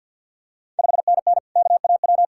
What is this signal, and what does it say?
Morse code, heard as a keyed tone, about 53 wpm, 690 Hz. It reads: 5NN KAW
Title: Ontario QSO Party Exchange 50wpm